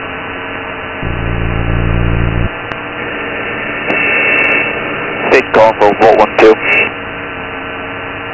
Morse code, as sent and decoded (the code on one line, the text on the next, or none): none